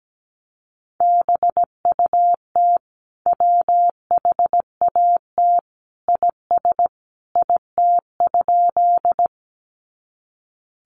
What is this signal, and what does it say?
Morse code, heard as a keyed tone, about 17 wpm, 700 Hz. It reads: BUT WHAT IS IT?
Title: Sentences from Top 100 Words 17wpm